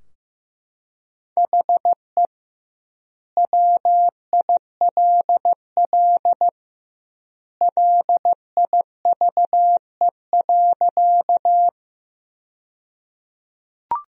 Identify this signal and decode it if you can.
Morse code, heard as a keyed tone, about 15 wpm, 700 Hz. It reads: HE WILL LIVE.
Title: 2x Word-Spacing - Sentences from Top 200 Words - 15wpm